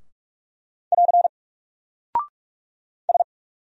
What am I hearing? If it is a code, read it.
Morse code, heard as a keyed tone, about 44 wpm, 700 Hz. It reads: <AR> S
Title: Single Character (Letter:Number:Punctuation:Prosign) 45wpm